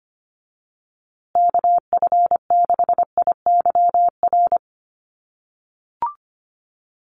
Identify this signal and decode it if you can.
Morse code, heard as a keyed tone, about 25 wpm, 700 Hz. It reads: KF6IYR
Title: USA Call Signs Repeated 3x 25wpm